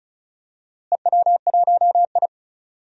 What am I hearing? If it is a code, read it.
Morse code, heard as a keyed tone, about 34 wpm, 700 Hz. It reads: EW1I
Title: DXCC-100 Call Signs - 4 Characters - Repeated 2x - 35wpm